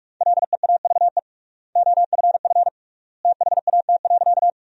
Morse code, heard as a keyed tone, about 43 wpm, 700 Hz. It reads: LEAVE OFF THAT.
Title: 3x Word-Spacing - Sentences from Top 400 Words - 45wpm